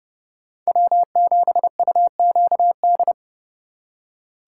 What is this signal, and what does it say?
Morse code, heard as a keyed tone, about 29 wpm, 700 Hz. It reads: W7UQD